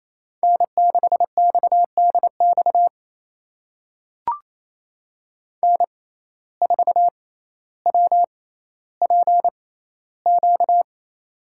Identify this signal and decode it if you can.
Morse code, heard as a keyed tone, about 28 wpm, spaced as for 14 wpm, 700 Hz. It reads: N6XDX N4WPQ
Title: USA Call Signs - 5 Characters - 6x Character Spacing - 28wpm